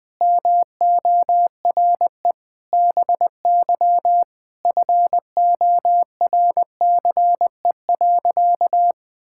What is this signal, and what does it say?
Morse code, heard as a keyed tone, about 20 wpm, 700 Hz. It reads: MORE BY FORCE.
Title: Sentences from Top 1000 Words - Review Part 6 - 20wpm